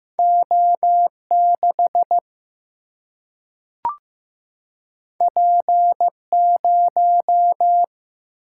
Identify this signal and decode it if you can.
Morse code, heard as a keyed tone, about 15 wpm, 700 Hz. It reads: O6 P0
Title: Single-Letter Single-Number 15wpm